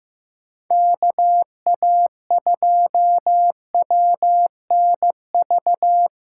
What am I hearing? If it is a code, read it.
Morse code, heard as a keyed tone, about 15 wpm, 700 Hz. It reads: KA2WNV
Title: USA Call Signs - 6 Characters - 8x Character Spacing - 15wpm